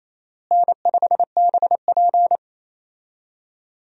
Morse code, heard as a keyed tone, about 28 wpm, 700 Hz. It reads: N5BP
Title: USA Call Signs - 4 Characters - Repeated 3x - 28wpm